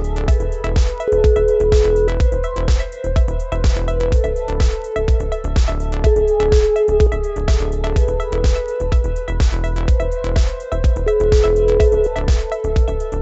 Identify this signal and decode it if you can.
Ringback tone (Europe, 425 Hz).